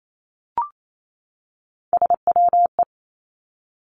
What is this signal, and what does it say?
Morse code, heard as a keyed tone, about 28 wpm, 700 Hz. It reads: SWE
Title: Three Challenging Letter-Number Combinations 28wpm